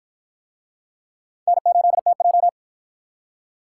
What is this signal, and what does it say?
Morse code, heard as a keyed tone, about 49 wpm, 700 Hz. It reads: N8TJ